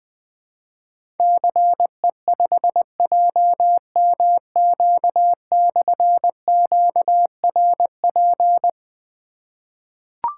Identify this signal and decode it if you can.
Morse code, heard as a keyed tone, about 20 wpm, 700 Hz. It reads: CE5JMQ/QRP
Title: Difficult Call Signs 20wpm